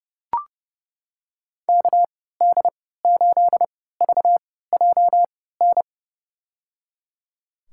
Morse code, heard as a keyed tone, about 30 wpm, spaced as for 17 wpm, 700 Hz. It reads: KD8VJN
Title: USA Call Signs - 3x Character Spacing - 30wpm